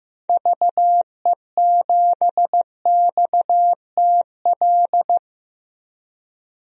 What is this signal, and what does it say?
Morse code, heard as a keyed tone, about 15 wpm, 700 Hz. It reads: VE7XTL
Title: DXCC-100 Call Signs - 6 Characters - Repeated 3x - 15wpm